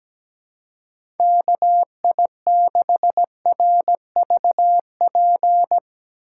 Morse code, heard as a keyed tone, about 17 wpm, 700 Hz. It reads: KI6RVP
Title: USA Call Signs - 3x Character Spacing - 17wpm